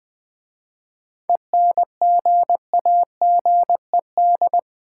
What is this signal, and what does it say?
Morse code, heard as a keyed tone, about 20 wpm, 700 Hz. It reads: ENGAGED
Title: 7 Letter Words - First 3 Letters Given - 20wpm